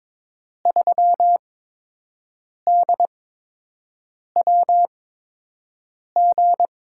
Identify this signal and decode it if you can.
Morse code, heard as a keyed tone, about 22 wpm, spaced as for 6 wpm, 700 Hz. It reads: 3DWG
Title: USA Call Signs - 8x Character Spacing - 22wpm